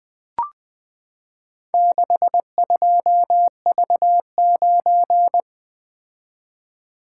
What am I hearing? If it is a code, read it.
Morse code, heard as a keyed tone, about 20 wpm, 700 Hz. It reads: 62V9
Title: Four Letter-Number Combinations - 20wpm